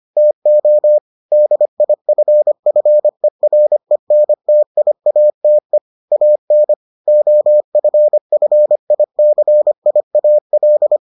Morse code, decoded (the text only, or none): TO DIFFERENTIATE AN OFFICIAL